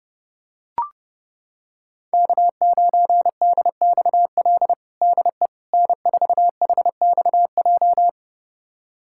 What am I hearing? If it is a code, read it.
Morse code, heard as a keyed tone, about 32 wpm, 700 Hz. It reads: K9DXL DE N4HXJ
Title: USA Callsign DE Callsign - 5 Characters - 30wpm